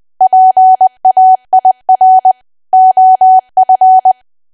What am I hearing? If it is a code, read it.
Morse code, heard as a keyed tone, about 20 wpm, 750 Hz. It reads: PAIR OF
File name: cwa / W1AW-Level3 / 20WPM